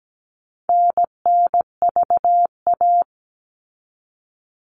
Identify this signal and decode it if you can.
Morse code, heard as a keyed tone, about 17 wpm, 700 Hz. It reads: NNVA
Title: All Cut Numbers - 4-Digit 17wpm